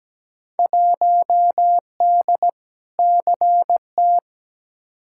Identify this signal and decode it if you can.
Morse code, heard as a keyed tone, about 17 wpm, 700 Hz. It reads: 1D CT
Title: Field Day Exchanges - 17wpm